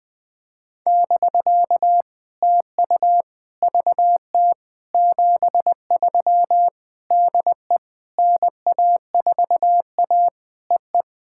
Morse code, heard as a keyed tone, about 20 wpm, 700 Hz. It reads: <BK> TU VT 73 DE NA4A EE
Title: POTA QSO Practice - 20wpm